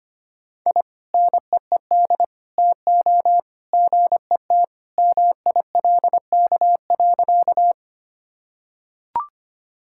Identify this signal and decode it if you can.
Morse code, heard as a keyed tone, about 25 wpm, 700 Hz. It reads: I NEED TO GET MILK.
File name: practice / 25wpm / Sentences with 2-Word Phrases - 25wpm